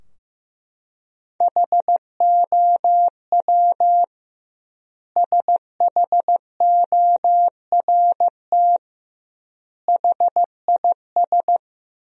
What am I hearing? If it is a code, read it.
Morse code, heard as a keyed tone, about 15 wpm, 700 Hz. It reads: HOW SHORT HIS
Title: 2x Word-Spacing - Sentences from Top 400 Words - 15wpm